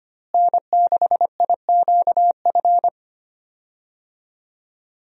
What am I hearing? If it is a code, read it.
Morse code, heard as a keyed tone, about 25 wpm, 700 Hz. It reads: N6IQF
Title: USA Call Signs - 5 Characters - Repeated 3x - 25wpm